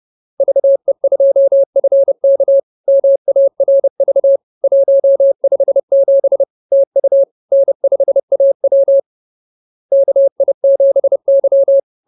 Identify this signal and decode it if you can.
Morse code, heard as a keyed tone, about 30 wpm, 550 Hz. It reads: VE2FK MARV 157 TU N5AW KI7Y
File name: CWT-215-30